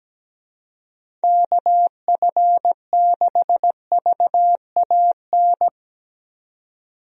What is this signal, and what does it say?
Morse code, heard as a keyed tone, about 17 wpm, 700 Hz. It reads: KF6VAN
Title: USA Call Signs - 6 Characters - 17wpm